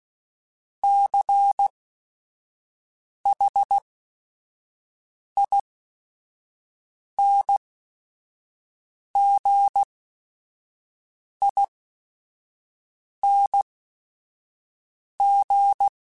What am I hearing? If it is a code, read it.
Morse code, heard as a keyed tone, about 16 wpm, spaced as for 5 wpm, 800 Hz. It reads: CHINGING